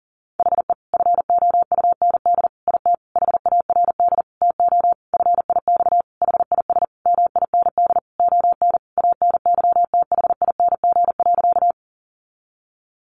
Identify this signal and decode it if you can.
Morse code, heard as a keyed tone, about 40 wpm, 700 Hz. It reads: HE FOUND IT HARD TO FIX HIS MIND ON ANYTHING.
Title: Sentences from Top 1400 Words 40wpm